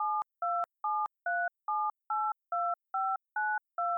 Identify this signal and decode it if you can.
DTMF, heard as telephone touch tones, tones of about 250 ms. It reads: 7273782592